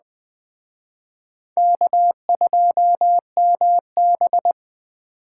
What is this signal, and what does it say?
Morse code, heard as a keyed tone, about 20 wpm, 700 Hz. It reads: K2MB
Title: Speed Racing - USA 1x2 Call Signs 17wpm